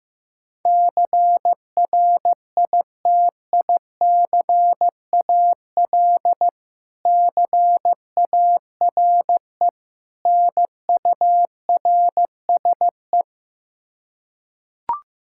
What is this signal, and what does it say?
Morse code, heard as a keyed tone, about 15 wpm, 700 Hz. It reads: CRITICAL CARE NURSE
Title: Practice QSO - Exchange 3 - More Info - 15wpm